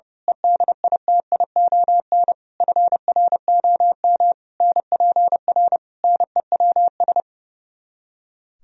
Morse code, heard as a keyed tone, about 30 wpm, 700 Hz. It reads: EDITION FROM NPR NEWS
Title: Speed-Racing - Sets of 5 Words 30wpm